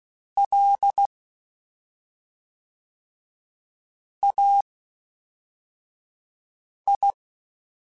Morse code, heard as a keyed tone, about 16 wpm, spaced as for 3 wpm, 800 Hz. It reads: LAI
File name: CW Comparisons / CW Comparisons-mp3s / CW Comparisons Broken Characters @5